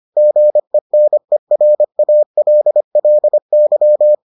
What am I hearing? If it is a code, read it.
Morse code, heard as a keyed tone, about 25 wpm, 600 Hz. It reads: GENERALLY